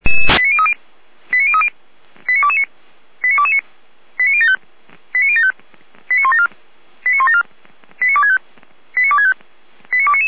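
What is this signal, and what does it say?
Five-tone selective call (ZVEI), heard as quick runs of five tones, tones of about 70 ms. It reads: 89030 89030 89229 89229 89075 89075 89174 89174 89265 89265 89299
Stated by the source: ZVEI five-tone call received via SDR, archived by Feuerwehr Orlamünde fire brigade.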